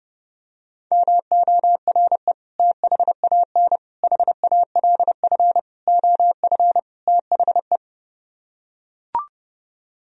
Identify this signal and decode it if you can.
Morse code, heard as a keyed tone, about 30 wpm, 700 Hz. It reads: MORE THAN HALF OF THE